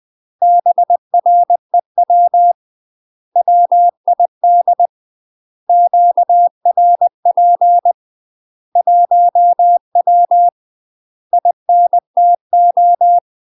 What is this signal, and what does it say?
Morse code, heard as a keyed tone, about 20 wpm, 700 Hz. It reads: BREW WID QRP 1W INTO
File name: UK MASTER QSOs TRAINING / 20WPM / UK MASTER QSOs 0007